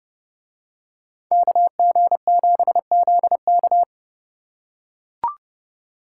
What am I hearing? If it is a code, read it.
Morse code, heard as a keyed tone, about 29 wpm, 700 Hz. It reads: KG7ZK